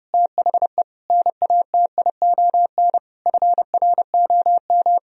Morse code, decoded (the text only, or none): THE NATION FROM